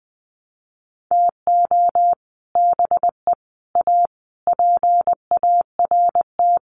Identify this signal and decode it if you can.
Morse code, heard as a keyed tone, about 19 wpm, 700 Hz. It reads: TO BE A PART
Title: Sets of 5 Words 20wpm